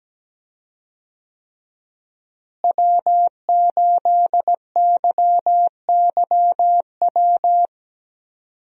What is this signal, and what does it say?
Morse code, heard as a keyed tone, about 17 wpm, 700 Hz. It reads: W8YYW
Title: USA Call Signs - 5 Characters - Repeated 3x - 17wpm